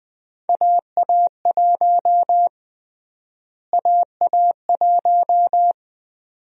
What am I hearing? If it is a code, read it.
Morse code, heard as a keyed tone, about 20 wpm, 700 Hz. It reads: AA1 AA1
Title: USA Call Sign Prefixes - Repeated 2x - 20wpm